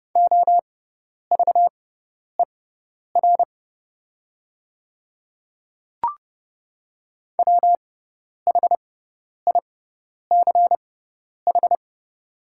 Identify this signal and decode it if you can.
Morse code, heard as a keyed tone, about 30 wpm, spaced as for 10 wpm, 700 Hz. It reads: OVER WHICH
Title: 6x Character Spacing - Top 100 Words - 30wpm